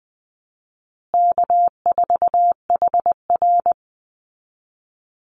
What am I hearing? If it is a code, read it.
Morse code, heard as a keyed tone, about 20 wpm, 700 Hz. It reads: K4HR
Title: Call Signs Repeated 2x - 20wpm